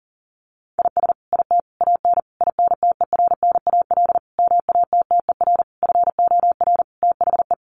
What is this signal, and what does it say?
Morse code, heard as a keyed tone, about 40 wpm, 700 Hz. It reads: IS IT AN INTERNAL MATTER FOR THE
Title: Sentences from Top 1400 Words 40wpm